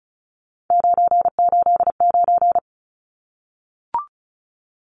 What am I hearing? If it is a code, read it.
Morse code, heard as a keyed tone, about 34 wpm, 700 Hz. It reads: 989
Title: Numbers - 3-Digit 35wpm